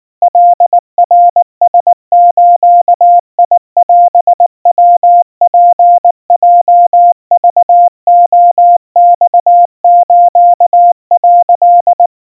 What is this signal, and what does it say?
Morse code (Japanese Wabun code), heard as a keyed tone, about 19 wpm, 700 Hz. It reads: カナラズオヤツヲクレマス」
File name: kawa / Morse sample / kamyu 70 cpm 700 Hz Crispy